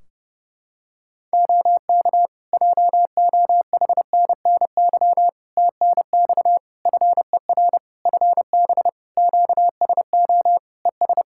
Morse code, heard as a keyed tone, about 30 wpm, 700 Hz. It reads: OK JOHNNY TNX FER FB QSO ES
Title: Practice QSO - Exchange 4 - Ending - 30wpm